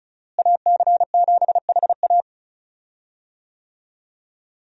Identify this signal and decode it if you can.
Morse code, heard as a keyed tone, about 36 wpm, 700 Hz. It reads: AC7HA
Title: USA Call Signs - 5 Characters - Repeated 2x - 35wpm